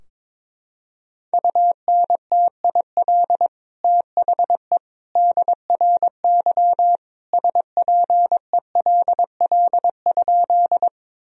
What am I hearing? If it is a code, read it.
Morse code, heard as a keyed tone, about 22 wpm, 700 Hz. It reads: UNTIL THE DRY SPELL?
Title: Sentences from Top 1000 Words - Review Part 3 - 22wpm